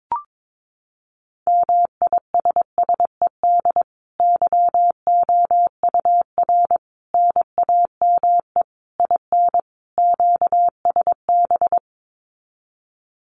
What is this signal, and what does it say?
Morse code, heard as a keyed tone, about 22 wpm, 700 Hz. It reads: MISSED YOUR NAME IN QSB
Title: Common QSO Phrases 22wpm